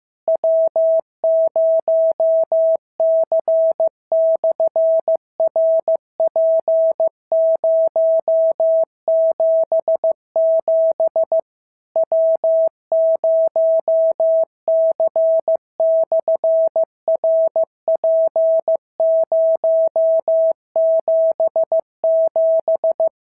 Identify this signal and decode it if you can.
Morse code, heard as a keyed tone, about 15 wpm, 650 Hz. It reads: W0C/RP077 W0C/RP077
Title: SOTA QSO Practice 15wpm